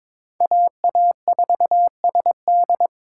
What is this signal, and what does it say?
Morse code, heard as a keyed tone, about 22 wpm, 700 Hz. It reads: AA4SD
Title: USA Call Signs - 5 Characters - 2x Character Spacing - 22wpm